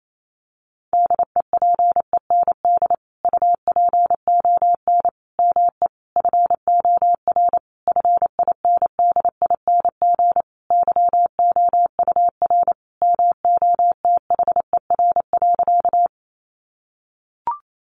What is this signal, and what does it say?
Morse code, heard as a keyed tone, about 28 wpm, 700 Hz. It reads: DEPEND UPON ME FOR FINDING YOUR MOTHER.
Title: Sentences from Top 1300 Words 28wpm